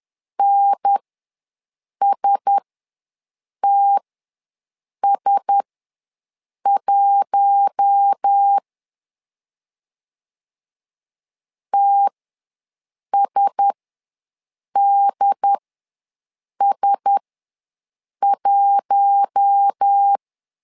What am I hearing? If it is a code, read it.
Morse code, heard as a keyed tone, about 10 wpm, spaced as for 6 wpm, 800 Hz. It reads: NSTS1 TSDS1
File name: LEZIONI / 4 Q1S ws 10wpm 2208